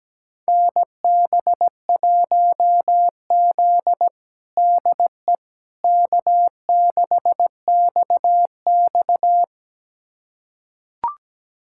Morse code, heard as a keyed tone, about 17 wpm, 700 Hz. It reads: NB1Z DE K6XX
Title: USA Callsign DE Callsign - 4 Characters - 17wpm